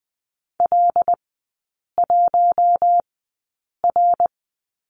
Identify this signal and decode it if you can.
Morse code, heard as a keyed tone, about 20 wpm, spaced as for 8 wpm, 700 Hz. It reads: L1R
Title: 121 Lesson 20e - Number 3 - Three Characters 20wpm